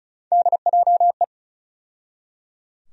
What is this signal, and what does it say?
Morse code, heard as a keyed tone, about 34 wpm, 700 Hz. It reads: DJE
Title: USA Call Sign Postfixes - 3 Letters 35wpm